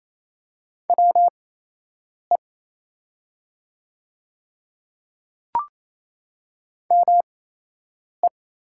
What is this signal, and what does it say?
Morse code, heard as a keyed tone, about 28 wpm, spaced as for 8 wpm, 700 Hz. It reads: WE ME
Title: 8x Character Spacing - Top 200 Words - 28wpm